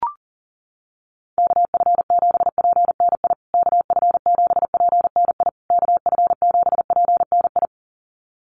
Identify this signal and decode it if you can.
Morse code, heard as a keyed tone, about 40 wpm, 700 Hz. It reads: KF7PNI KF7PNI KF7PNI
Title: USA Call Signs Repeated 3x 40wpm